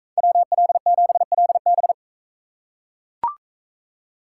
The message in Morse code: WL7LB

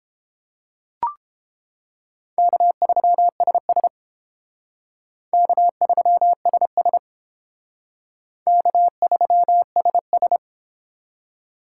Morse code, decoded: K3SS K3SS K3SS

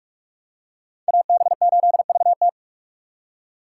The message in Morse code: AB8VT

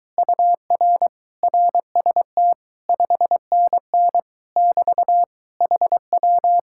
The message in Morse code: UR RST 5NN <BT> HW

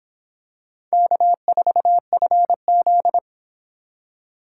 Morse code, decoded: K4FZ

